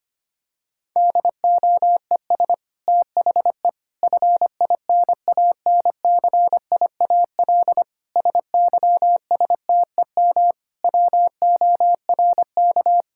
Morse code, heard as a keyed tone, about 25 wpm, 700 Hz. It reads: DOES THE FINANCIAL SYSTEM WORK
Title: Sentences from Top 1000 Words - Review Part 5 - 25wpm